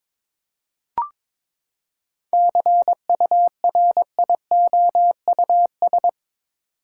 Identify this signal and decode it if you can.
Morse code, heard as a keyed tone, about 22 wpm, 700 Hz. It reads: CURIOUS
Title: Seven Letter Words - 22wpm